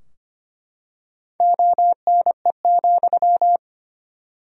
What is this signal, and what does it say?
Morse code, heard as a keyed tone, about 25 wpm, 700 Hz. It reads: ONE,
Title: 3x Word-Spacing - Sentences from Top 200 Words - 25wpm